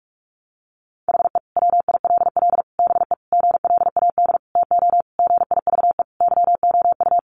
Morse code, heard as a keyed tone, about 44 wpm, 700 Hz. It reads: HE WILL BE GLAD TO GIVE YOU
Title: Sentences from Top 800 Words 45wpm